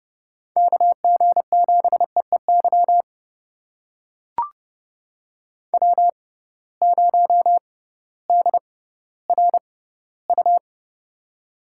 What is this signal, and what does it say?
Morse code, heard as a keyed tone, about 31 wpm, spaced as for 16 wpm, 700 Hz. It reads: KG7EEY W0DRU
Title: USA Call Signs - 6x Character Spacing - 30wpm